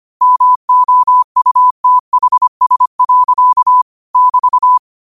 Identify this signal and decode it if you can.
Morse code, heard as a keyed tone, about 25 wpm, 1000 Hz. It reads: MOUTHS. <BT>
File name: qotdfiles / 25wpm